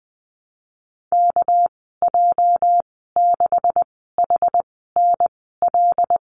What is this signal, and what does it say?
Morse code, heard as a keyed tone, about 19 wpm, spaced as for 14 wpm, 700 Hz. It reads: KJ6HNL
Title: 231 Lesson 37g - Letter x - Call Signs 20wpm